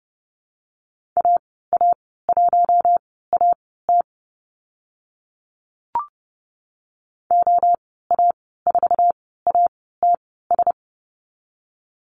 Morse code, heard as a keyed tone, about 30 wpm, spaced as for 17 wpm, 700 Hz. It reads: AA1AT OA4ATS
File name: Lesson 15g - Number 5 - Call Signs 30wpm